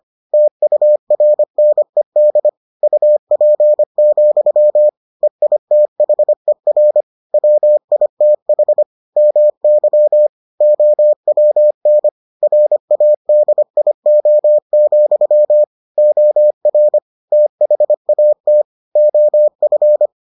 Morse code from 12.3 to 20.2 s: RADIO, OR THAT OF

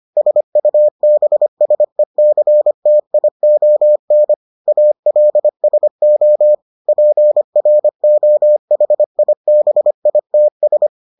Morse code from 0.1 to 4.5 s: SUBSECTION

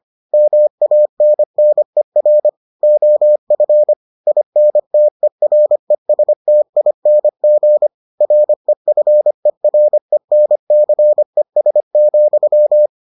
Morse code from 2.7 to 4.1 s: OF

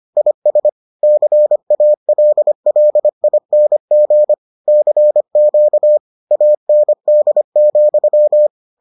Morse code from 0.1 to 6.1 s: IS CALLING CQ

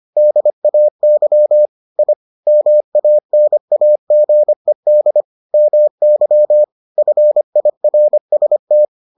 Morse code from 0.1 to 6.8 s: DAY I MANAGED MY

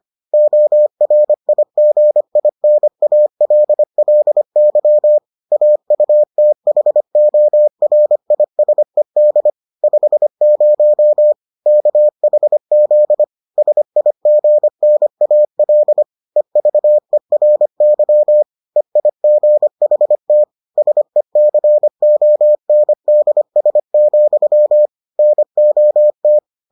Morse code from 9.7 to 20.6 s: 50 KHZ SIGNAL EVERY EIGHT

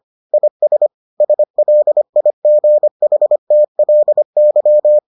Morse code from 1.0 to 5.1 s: SLIGHTLY

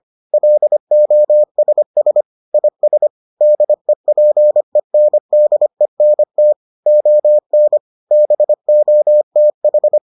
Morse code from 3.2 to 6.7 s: DEPENDENT

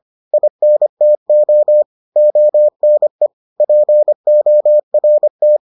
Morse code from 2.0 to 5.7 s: ONE PORT

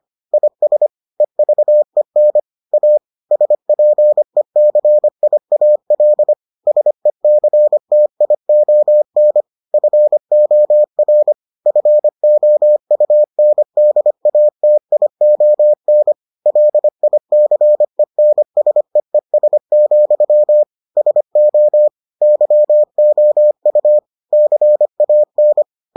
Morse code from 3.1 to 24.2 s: SPECIAL SECTION FOR FOUNDATION LICENSEES, SO YOU

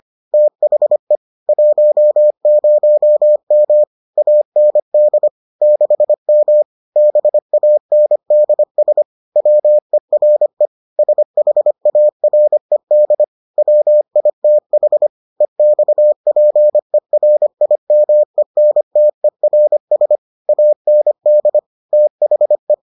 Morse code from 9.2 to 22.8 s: WERE SHARED WITH EXPERIMENTERS AND THE

